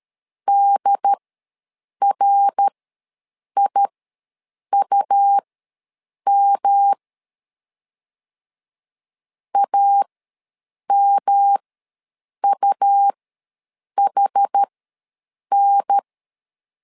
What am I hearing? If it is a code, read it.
Morse code, heard as a keyed tone, about 13 wpm, spaced as for 7 wpm, 800 Hz. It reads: DRIUM AMUHN
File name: LEZIONI / 3 MWRO / 3 MWR ws 12wpm 1809